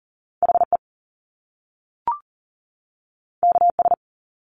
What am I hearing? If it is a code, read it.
Morse code, heard as a keyed tone, about 38 wpm, 700 Hz. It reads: HE KS
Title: Top Two Letter Combinations in Words 40wpm